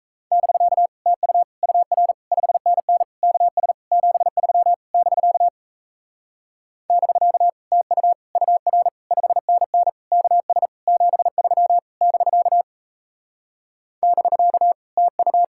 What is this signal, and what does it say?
Morse code, heard as a keyed tone, about 37 wpm, 700 Hz. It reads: <BK> TU UR 5NN KS 73 <BK> <BK> TU UR 5NN KS 73 <BK> <BK> TU